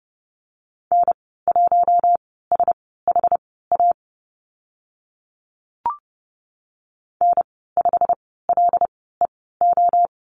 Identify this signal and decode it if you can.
Morse code, heard as a keyed tone, about 30 wpm, spaced as for 17 wpm, 700 Hz. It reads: N1SHA N5LEO
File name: Lesson 15g - Number 5 - Call Signs 30wpm